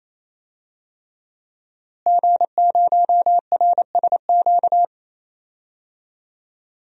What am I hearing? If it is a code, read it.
Morse code, heard as a keyed tone, about 28 wpm, 700 Hz. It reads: G0RSQ
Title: DXCC-100 Call Signs - 5 Characters - Repeated 3x - 28wpm